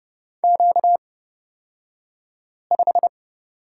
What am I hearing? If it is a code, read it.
Morse code, heard as a keyed tone, about 30 wpm, spaced as for 5 wpm, 700 Hz. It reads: Q5
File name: Lesson 38d - Letter q - Two Characters 30wpm